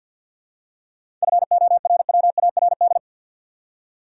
Morse code, heard as a keyed tone, about 50 wpm, 700 Hz. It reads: FORWARD
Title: Seven Letter Words - 50wpm